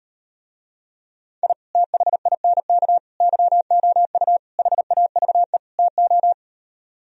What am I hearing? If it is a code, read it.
Morse code, heard as a keyed tone, about 38 wpm, 700 Hz. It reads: I THINK YOU HAVE TO